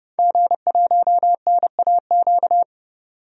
Morse code, heard as a keyed tone, about 30 wpm, 700 Hz. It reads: G1NAQ
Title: DXCC-100 Call Signs - 5 Characters - 30wpm